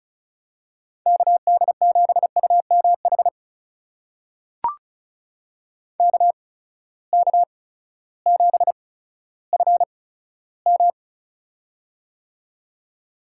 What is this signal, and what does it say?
Morse code, heard as a keyed tone, about 36 wpm, spaced as for 16 wpm, 700 Hz. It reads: KD7UMH KK7FM
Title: USA Call Signs - 8x Character Spacing - 35wpm